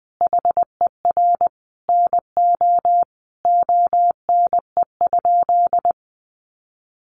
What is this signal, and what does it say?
Morse code, heard as a keyed tone, about 20 wpm, 700 Hz. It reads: HER NO ONE?
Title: Sentences from Top 600 Words 20wpm